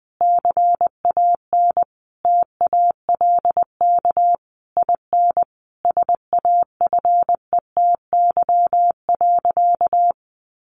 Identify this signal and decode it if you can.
Morse code, heard as a keyed tone, about 20 wpm, 700 Hz. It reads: CAN TALK IN SAFETY.
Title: Sentences from Top 700 Words 20wpm